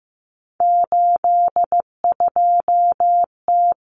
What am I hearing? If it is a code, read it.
Morse code, heard as a keyed tone, about 15 wpm, 700 Hz. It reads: 82T
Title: Common Cut Numbers - 3-Digit 15wpm